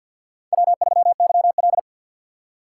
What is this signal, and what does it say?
Morse code, heard as a keyed tone, about 64 wpm, 700 Hz. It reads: W3YL